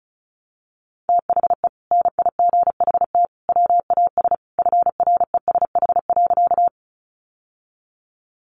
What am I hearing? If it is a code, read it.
Morse code, heard as a keyed tone, about 34 wpm, 700 Hz. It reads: THE NIGHT WAS FRESH.
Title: Sentences from Top 1200 Words 35wpm